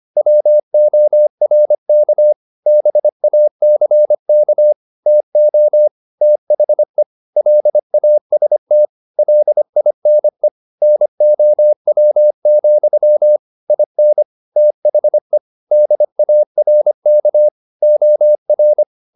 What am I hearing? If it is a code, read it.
Morse code, heard as a keyed tone, about 25 wpm, 600 Hz. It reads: WORK BACK TO THE LAST LINE NOW, IN THE DARK OR
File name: the old man and the sea part 1 25wpm